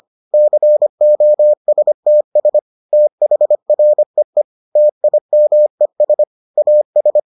Morse code, heard as a keyed tone, about 25 wpm, 600 Hz. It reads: COSTS THREE TIMES AS